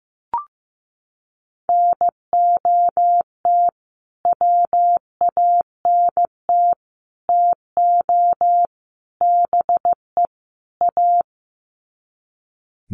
Morse code, heard as a keyed tone, about 15 wpm, 700 Hz. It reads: NOT WANT TO BE A
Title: Sets of 5 Words Encore 15wpm